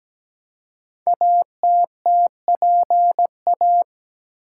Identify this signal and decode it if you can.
Morse code, heard as a keyed tone, about 17 wpm, 700 Hz. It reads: ATTPA
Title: Code Groups - 5 Letters - 17wpm